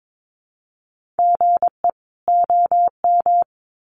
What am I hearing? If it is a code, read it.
Morse code, heard as a keyed tone, about 22 wpm, 700 Hz. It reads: GE OM
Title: Common QSO Phrases 22wpm